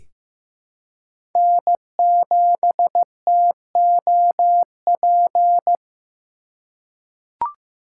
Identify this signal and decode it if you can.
Morse code, heard as a keyed tone, about 15 wpm, 700 Hz. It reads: N7TOP
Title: USA Call Signs - 2x Character Spacing - 15wpm